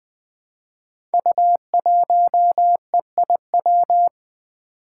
Morse code, heard as a keyed tone, about 20 wpm, 700 Hz. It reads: U1EIW
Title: Five Challenging Letter-Number Combinations 20wpm